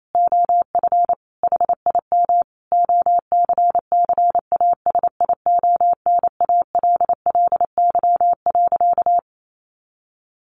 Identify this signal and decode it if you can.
Morse code, heard as a keyed tone, about 28 wpm, 700 Hz. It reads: OF HIM OCCASIONALLY.
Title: Sentences from Top 1300 Words 28wpm